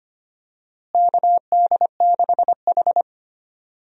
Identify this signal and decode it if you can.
Morse code, heard as a keyed tone, about 25 wpm, 700 Hz. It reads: KD6H
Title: USA Call Signs - 4 Characters - Repeated 3x - 25wpm